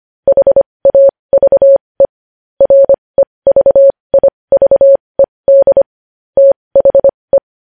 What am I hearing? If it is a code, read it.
Morse code, heard as a keyed tone, about 25 wpm, spaced as for 20 wpm, 550 Hz. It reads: HAVE REVIVED THE
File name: podcast / News headlines-1x-25x20